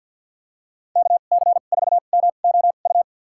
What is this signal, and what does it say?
Morse code, heard as a keyed tone, about 47 wpm, spaced as for 35 wpm, 690 Hz. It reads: KC4MOU
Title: USA Call Signs - 2x Character Spacing - 50wpm